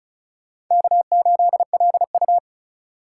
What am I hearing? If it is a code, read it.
Morse code, heard as a keyed tone, about 45 wpm, 700 Hz. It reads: K8LU